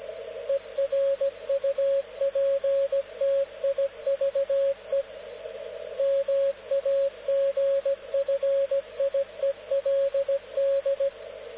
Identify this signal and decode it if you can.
Morse code, heard as a keyed tone, about 17 wpm, 550 Hz, with radio noise. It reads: ERUPTIVE MAGFIELD
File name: luxorion / Radio / dk0wcy-spaceweather-10144khz